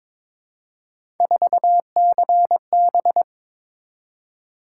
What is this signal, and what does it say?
Morse code, heard as a keyed tone, about 23 wpm, 700 Hz. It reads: 4CB
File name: practice / 22wpm / Three Letter-Number Combinations - 22wpm